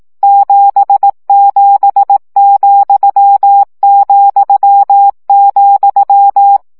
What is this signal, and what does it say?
Morse code, heard as a keyed tone, about 18 wpm, 800 Hz. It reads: 77,,,
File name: CW gyakorlas z 7 vesszo 15 baud